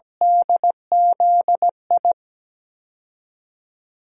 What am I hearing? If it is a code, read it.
Morse code, heard as a keyed tone, about 17 wpm, 700 Hz. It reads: DZI
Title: DXCC-100 Call Signs - 6 Characters - 17wpm